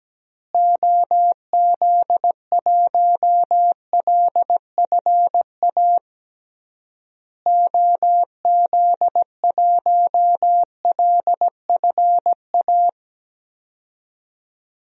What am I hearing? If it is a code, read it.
Morse code, heard as a keyed tone, about 17 wpm, 700 Hz. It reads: OZ1LFA OZ1LFA
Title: DXCC-100 Call Signs - 6 Characters - Repeated 3x - 17wpm